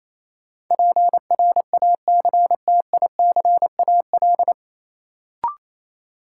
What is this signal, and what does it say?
Morse code, heard as a keyed tone, about 28 wpm, 700 Hz. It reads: PRACTICAL
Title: Sentences from Top 1400 Words - New Word Drill Alt - 28wpm